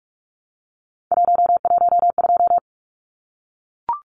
Morse code, heard as a keyed tone, about 44 wpm, 700 Hz. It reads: 112
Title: Numbers - 3-Digit 45wpm